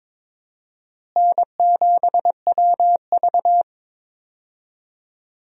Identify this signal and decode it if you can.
Morse code, heard as a keyed tone, about 22 wpm, 700 Hz. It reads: N7WV
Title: USA Call Signs - 4 Characters - Repeated 2x - 22wpm